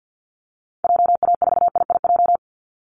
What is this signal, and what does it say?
Morse code, heard as a keyed tone, about 50 wpm, 700 Hz. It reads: JA4IIJ
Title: Call Signs Repeated 3x - 50wpm